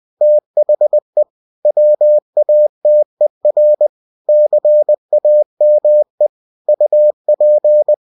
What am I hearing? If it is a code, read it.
Morse code, heard as a keyed tone, about 20 wpm, 600 Hz. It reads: THE WATER CAME UP
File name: cw-books / winnie-the-pooh / winnie-20wpm-page-0096